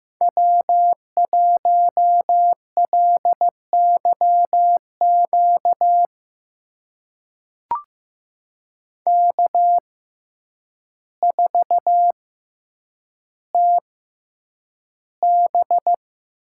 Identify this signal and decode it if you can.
Morse code, heard as a keyed tone, about 15 wpm, spaced as for 8 wpm, 700 Hz. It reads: W1LYQ K4TB
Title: USA Call Signs - 5 Characters - 6x Character Spacing - 15wpm